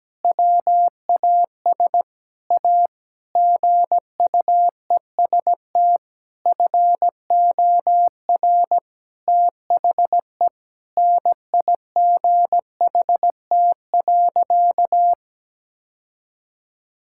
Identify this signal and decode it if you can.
Morse code, heard as a keyed tone, about 17 wpm, 700 Hz. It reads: WAS A GUEST FOR THE NIGHT.
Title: Sentences from Top 1600 Words 17wpm